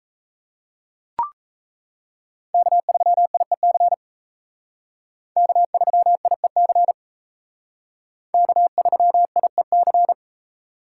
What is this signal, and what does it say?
Morse code, heard as a keyed tone, about 33 wpm, 700 Hz. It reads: K3IEC K3IEC K3IEC
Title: Speed Racing - USA 1x3 Call Signs 28wpm